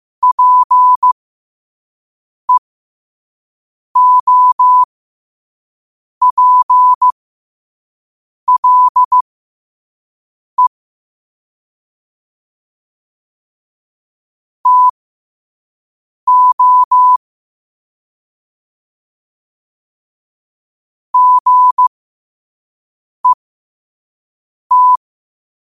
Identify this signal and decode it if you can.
Morse code, heard as a keyed tone, about 15 wpm, spaced as for 5 wpm, 1000 Hz. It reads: PEOPLE TO GET
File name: qotdfiles / 5wpm